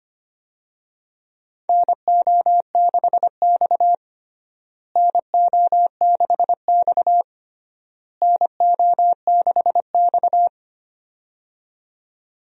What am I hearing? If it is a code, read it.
Morse code, heard as a keyed tone, about 25 wpm, 700 Hz. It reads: NO6X NO6X NO6X
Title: USA Call Signs - 4 Characters - Repeated 3x - 25wpm